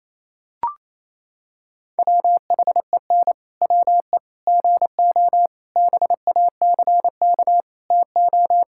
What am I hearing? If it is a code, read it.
Morse code, heard as a keyed tone, about 28 wpm, 700 Hz. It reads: WHEN WE GO BACK TO
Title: Pangram Challenge v2 - 28wpm